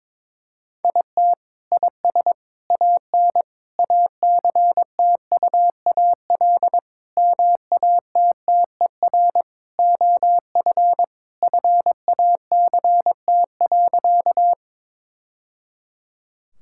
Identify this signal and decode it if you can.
Morse code, heard as a keyed tone, about 22 wpm, 700 Hz. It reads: IT IS AN ACTUAL MATTER OF FACT.
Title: Sentences from Top 1000 Words - Review Part 5 - 22wpm